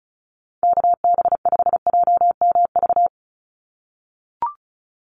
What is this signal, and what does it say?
Morse code, heard as a keyed tone, about 34 wpm, 700 Hz. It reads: KB5JMV
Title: USA Call Signs 35wpm